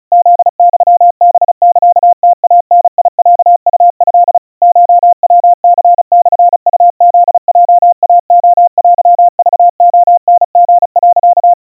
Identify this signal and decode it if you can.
Morse code (Japanese Wabun code), heard as a keyed tone, about 36 wpm, 700 Hz. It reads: フユハサムイダロウト コヤニモウフヲイレテクレタリ、
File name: kawa / Morse sample / kamyu 130 cpm 700 Hz Crispy